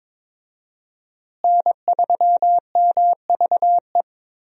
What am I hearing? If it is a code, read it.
Morse code, heard as a keyed tone, about 22 wpm, 700 Hz. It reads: N3MVE
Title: USA Call Signs - 5 Characters - Repeated 3x - 22wpm